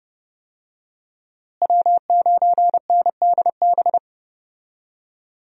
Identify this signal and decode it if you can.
Morse code, heard as a keyed tone, about 30 wpm, 700 Hz. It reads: W9NDB
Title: USA Call Signs - 5 Characters - Repeated 3x - 30wpm